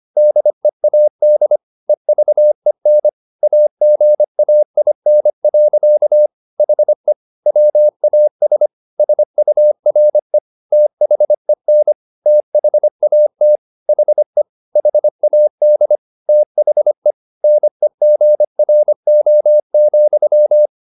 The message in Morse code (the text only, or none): DEAD EVEN AGAIN. HE WAS SURE THEN THAT HE HAD THE NEGRO,